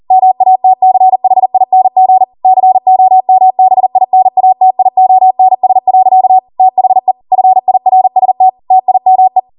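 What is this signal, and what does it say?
Morse code, heard as a keyed tone, about 40 wpm, 750 Hz. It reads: MATCHING COMBINATIONS. THE FIRST TIME
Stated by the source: Code practice at 40 wpm.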